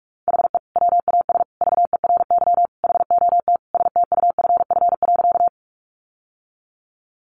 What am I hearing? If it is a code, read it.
Morse code, heard as a keyed tone, about 44 wpm, 700 Hz. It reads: HE WAS VERY HOT STUFF.
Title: Sentences from Top 1400 Words 45wpm